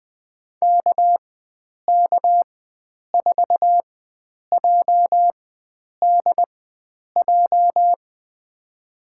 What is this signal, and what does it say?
Morse code, heard as a keyed tone, about 20 wpm, spaced as for 9 wpm, 700 Hz. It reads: KK4JDJ
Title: USA Call Signs - 4x Character Spacing - 20wpm